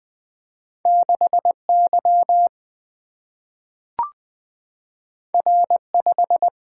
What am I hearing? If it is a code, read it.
Morse code, heard as a keyed tone, about 20 wpm, 700 Hz. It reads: 6Y R5
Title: Single-Letter Single-Number 20wpm